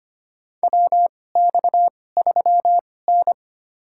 Morse code, heard as a keyed tone, about 25 wpm, spaced as for 18 wpm, 700 Hz. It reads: WX3N
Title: USA Call Signs - 4 Characters - 2x Character Spacing - 25wpm